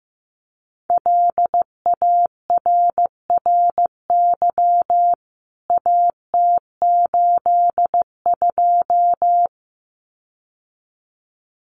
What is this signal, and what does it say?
Morse code, heard as a keyed tone, about 15 wpm, 700 Hz. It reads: LARRY AT82
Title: CWT Exchange 15wpm